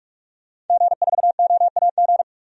Morse code, heard as a keyed tone, about 46 wpm, 700 Hz. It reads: G4OAG
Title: DXCC-100 Call Signs - 5 Characters - 45wpm